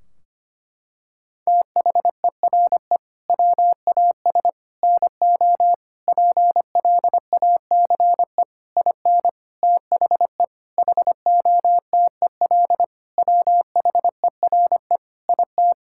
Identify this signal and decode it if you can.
Morse code, heard as a keyed tone, about 25 wpm, 700 Hz. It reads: THERE WAS NO PLACE IN THE HOTEL WHERE IT
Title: Sentences from Top 1600 Words 25wpm